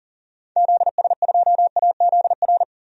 Morse code, heard as a keyed tone, about 40 wpm, 700 Hz. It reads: ZS2AZR
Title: DXCC-100 Call Signs - 6 Characters - 40wpm